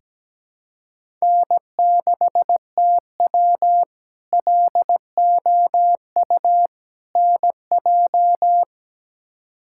Morse code, heard as a keyed tone, about 17 wpm, 700 Hz. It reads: N6TW LOU NJ
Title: K1USN SST QSO Practice - 17wpm